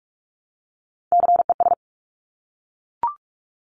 Morse code, heard as a keyed tone, about 46 wpm, 700 Hz. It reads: CES
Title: Top Three Letter Combinations in Words 45wpm